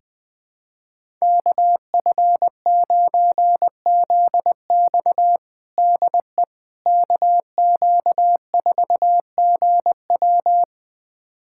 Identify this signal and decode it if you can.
Morse code, heard as a keyed tone, about 20 wpm, 700 Hz. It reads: KF9ZX DE KQ4GW
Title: USA Callsign DE Callsign - 5 Characters - 20wpm